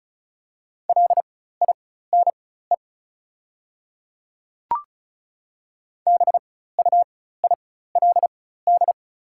Morse code, heard as a keyed tone, about 36 wpm, spaced as for 16 wpm, 700 Hz. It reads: LINE BUILD
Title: 4x Character Spacing - Top 200 Words - 35wpm